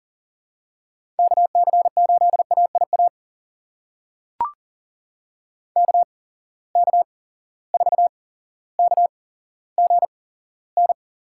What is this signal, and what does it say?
Morse code, heard as a keyed tone, about 38 wpm, spaced as for 17 wpm, 700 Hz. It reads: KC8AIA KK4KGN
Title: USA Call Signs - 6 Characters - 8x Character Spacing - 40wpm